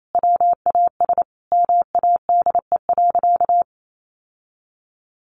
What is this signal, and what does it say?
Morse code, heard as a keyed tone, about 28 wpm, 700 Hz. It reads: WAS MADE.
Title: Sentences from Top 1400 Words 28wpm